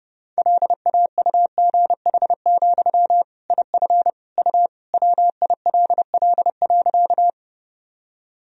Morse code, heard as a keyed tone, about 32 wpm, 700 Hz. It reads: LAUGH, IF U WILL.
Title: Sentences from Top 500 Words and Abbreviations 30wpm